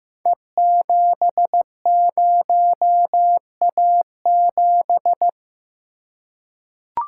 Morse code, heard as a keyed tone, about 15 wpm, 700 Hz. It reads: E70A7
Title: Code Groups - 5 Letter:Numbers - 15wpm